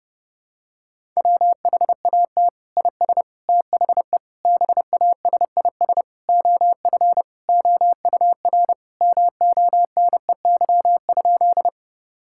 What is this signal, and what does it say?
Morse code, heard as a keyed tone, about 30 wpm, 700 Hz. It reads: WHAT IS THE BASIS OF OUR MONEY?
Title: Sentences from Top 1000 Words - Review Part 5 - 30wpm